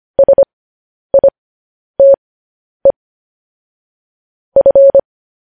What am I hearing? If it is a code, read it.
Morse code, heard as a keyed tone, about 25 wpm, spaced as for 10 wpm, 550 Hz. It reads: SITE F
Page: news-headlines-in-morse-code-at-10-wpm